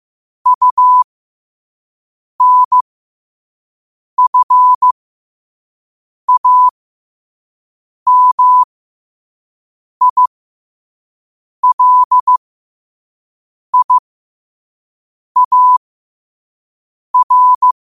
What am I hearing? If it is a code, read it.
Morse code, heard as a keyed tone, about 15 wpm, spaced as for 5 wpm, 1000 Hz. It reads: UNFAMILIAR
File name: qotdfiles / 5wpm